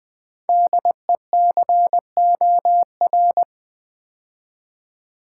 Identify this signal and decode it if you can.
Morse code, heard as a keyed tone, about 20 wpm, 700 Hz. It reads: DECOR
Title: Build A Word - 5 Letters At A Time - 10-Letter Words - 20wpm